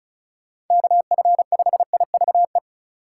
Morse code, heard as a keyed tone, about 36 wpm, 700 Hz. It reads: KF5IVE
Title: USA Call Signs - 3x Character Spacing - 35wpm